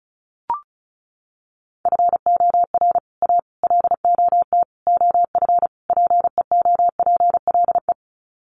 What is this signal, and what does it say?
Morse code, heard as a keyed tone, about 34 wpm, 700 Hz. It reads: FOR A LOT OF PEOPLE
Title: Sets of 5 Words Encore 35wpm